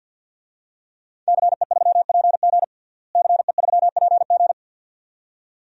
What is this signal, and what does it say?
Morse code, heard as a keyed tone, about 50 wpm, 700 Hz. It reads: CE3PG CE3PG
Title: DXCC-100 Call Signs - 5 Characters - Repeated 2x - 50wpm